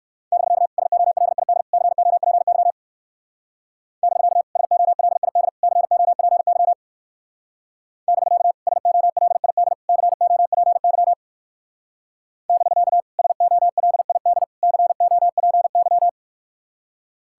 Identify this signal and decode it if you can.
Morse code, heard as a keyed tone, about 58 wpm, 700 Hz. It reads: <BK> SOLID COPY <BK> SOLID COPY <BK> SOLID COPY <BK> SOLID COPY